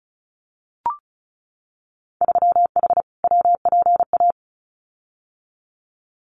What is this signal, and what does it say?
Morse code, heard as a keyed tone, about 36 wpm, 700 Hz. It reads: 3H WPA
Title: Winter Field Day Exchanges Encore 35wpm